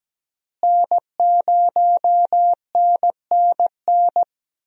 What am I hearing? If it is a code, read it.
Morse code, heard as a keyed tone, about 17 wpm, 700 Hz. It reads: N0NNN
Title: USA Call Signs - 5 Characters - Repeated 3x - 17wpm